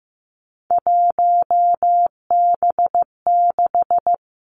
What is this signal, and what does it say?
Morse code, heard as a keyed tone, about 15 wpm, 700 Hz. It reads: 1B6
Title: Three Challenging Letter-Number Combinations 15wpm